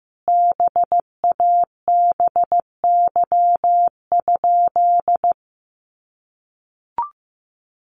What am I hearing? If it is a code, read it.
Morse code, heard as a keyed tone, about 15 wpm, 700 Hz. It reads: BABY?
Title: Sentences from Top 1300 Words 15wpm